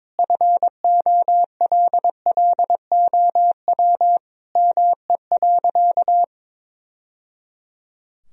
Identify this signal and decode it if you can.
Morse code, heard as a keyed tone, about 22 wpm, 700 Hz. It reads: FOLLOW ME.
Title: Sentences from Top 1000 Words - Review Part 1 - 22wpm